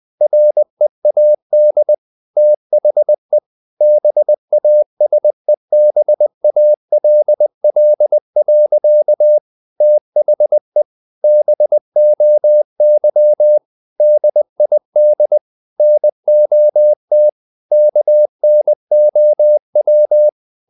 Morse code, heard as a keyed tone, about 20 wpm, 600 Hz. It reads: READ THE BASEBALL. THE BOY DID NOT KNOW
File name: the old man and the sea / the old man and the sea part 1 20wpm